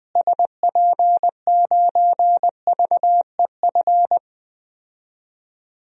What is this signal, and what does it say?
Morse code, heard as a keyed tone, about 21 wpm, 700 Hz. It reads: SP9VEF
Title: DXCC-100 Call Signs - 6 Characters - Repeated 2x - 20wpm